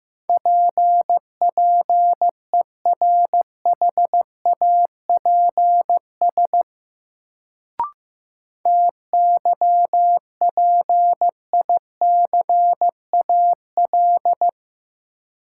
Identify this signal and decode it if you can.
Morse code, heard as a keyed tone, about 15 wpm, 700 Hz. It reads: P PERHAPS TYPICAL
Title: Sentences from Top 1100 Words - New Word Drill Alt - 15wpm